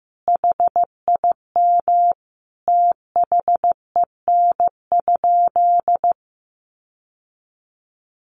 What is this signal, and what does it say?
Morse code, heard as a keyed tone, about 15 wpm, 700 Hz. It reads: HIM THEN?
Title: Sentences from Top 1300 Words 15wpm